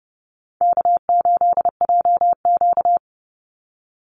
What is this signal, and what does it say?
Morse code, heard as a keyed tone, about 30 wpm, 700 Hz. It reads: K8JQ
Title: Easy Call Signs Repeated 2x - 30wpm